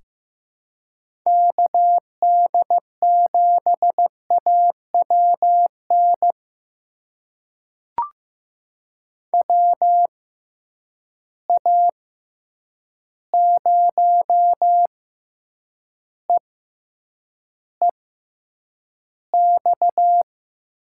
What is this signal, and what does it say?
Morse code, heard as a keyed tone, about 15 wpm, spaced as for 8 wpm, 700 Hz. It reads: KD7AWN WA0EEX
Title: USA Call Signs - 6 Characters - 6x Character Spacing - 15wpm